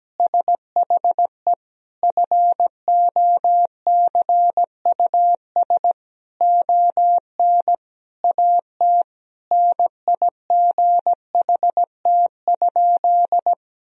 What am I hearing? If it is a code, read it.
Morse code, heard as a keyed tone, about 17 wpm, 700 Hz. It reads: SHE FOCUS ON AT NIGHT?
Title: Sentences from Top 1000 Words - Review Part 4 - 17wpm